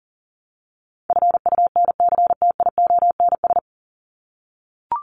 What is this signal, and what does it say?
Morse code, heard as a keyed tone, about 40 wpm, 700 Hz. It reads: FUNCTIONS
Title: ES:S Suffix - Top 100 Words with Base Words 40wpm